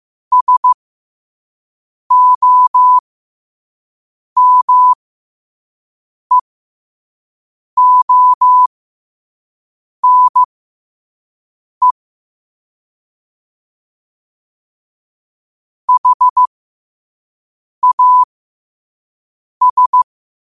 Morse code, heard as a keyed tone, about 15 wpm, spaced as for 5 wpm, 1000 Hz. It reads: SOMEONE HAS